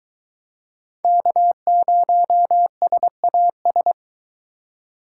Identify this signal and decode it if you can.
Morse code, heard as a keyed tone, about 22 wpm, 700 Hz. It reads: K0SAS